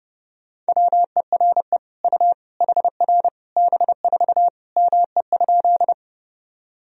Morse code, heard as a keyed tone, about 32 wpm, 700 Hz. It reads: WERE U HR B4 ME?
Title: Sentences from Top 200 Words and Abbreviations 30wpm